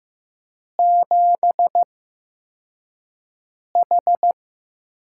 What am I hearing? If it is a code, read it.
Morse code, heard as a keyed tone, about 15 wpm, spaced as for 4 wpm, 700 Hz. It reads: 7H
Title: USA Call Signs - 4 Characters - 8x Character Spacing - 15wpm